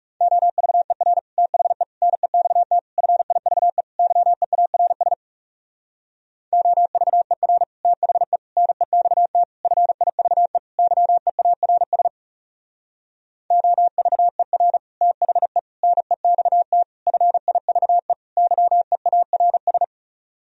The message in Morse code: OVER THE NEXT FIVE YEARS OVER THE NEXT FIVE YEARS OVER THE NEXT FIVE YEARS

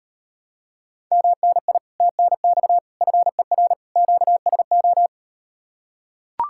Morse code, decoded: MNI TNX FER QSO

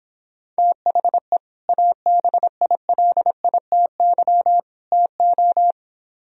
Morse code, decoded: THE ABILITY TO

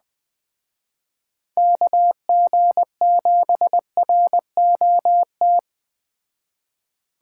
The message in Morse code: KG7ROT

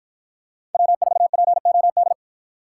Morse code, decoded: W4POD

